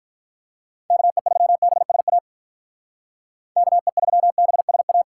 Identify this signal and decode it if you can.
Morse code, heard as a keyed tone, about 45 wpm, 700 Hz. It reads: KE3BSA KE3BSA